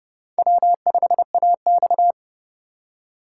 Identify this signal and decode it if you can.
Morse code, heard as a keyed tone, about 29 wpm, 700 Hz. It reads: W5AX